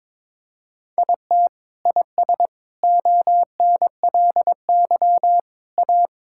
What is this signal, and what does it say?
Morse code, heard as a keyed tone, about 22 wpm, 700 Hz. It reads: IT IS ONLY A
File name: Sets of 9 Words - 22wpm